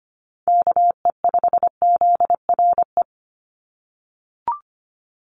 Morse code, heard as a keyed tone, about 25 wpm, 700 Hz. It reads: KE5ZRE
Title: USA Call Signs 25wpm